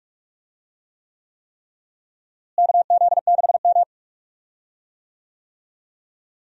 Morse code, heard as a keyed tone, about 47 wpm, 700 Hz. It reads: KZ6M
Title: USA Call Signs - 4 Characters - Repeated 3x - 45wpm